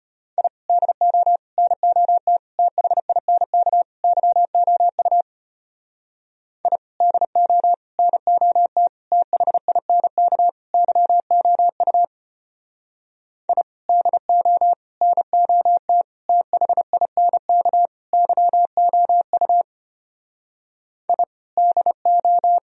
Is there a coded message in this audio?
Yes: Morse code from 0.4 to 22.6 s, heard as a keyed tone, about 32 wpm, 700 Hz. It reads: I DO NOT THINK YOU I DO NOT THINK YOU I DO NOT THINK YOU I DO